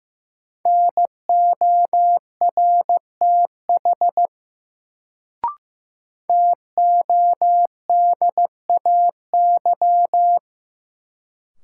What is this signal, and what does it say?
Morse code, heard as a keyed tone, about 16 wpm, 700 Hz. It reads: NORTH TODAY
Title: Sets of 5 Words Encore - New Word Drill - 15wpm